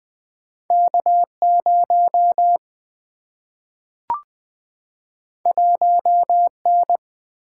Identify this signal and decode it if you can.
Morse code, heard as a keyed tone, about 20 wpm, 700 Hz. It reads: K0 1N
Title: Single-Letter Single-Number 20wpm